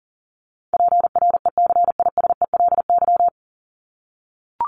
Morse code, heard as a keyed tone, about 40 wpm, 700 Hz. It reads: PRECISELY
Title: LY Suffix - Top 100 Words with Base Words 40wpm